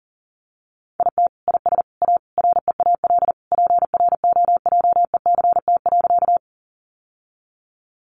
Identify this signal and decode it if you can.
Morse code, heard as a keyed tone, about 37 wpm, 700 Hz. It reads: IT IS A REAL PROJECT.
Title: Sentences from Top 1400 Words 40wpm